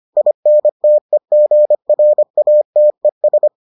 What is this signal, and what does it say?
Morse code, heard as a keyed tone, about 25 wpm, 600 Hz. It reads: INTEGRATES